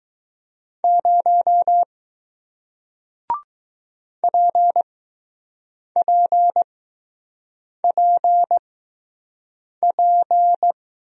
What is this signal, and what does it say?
Morse code, heard as a keyed tone, about 20 wpm, spaced as for 6 wpm, 700 Hz. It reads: 0 PPPP